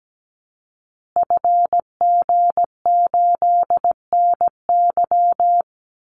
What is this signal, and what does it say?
Morse code, heard as a keyed tone, about 17 wpm, 700 Hz. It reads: FG8NY
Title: Call Signs Repeated 2x - 17wpm